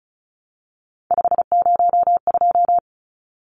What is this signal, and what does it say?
Morse code, heard as a keyed tone, about 34 wpm, 700 Hz. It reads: 502
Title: Numbers - 3-Digit 35wpm